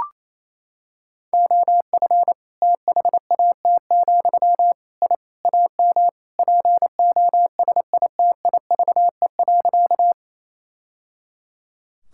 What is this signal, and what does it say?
Morse code, heard as a keyed tone, about 28 wpm, 700 Hz. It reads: OF THAT, I AM POSITIVE.
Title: Sentences from Top 1000 Words - Review Part 5 - 28wpm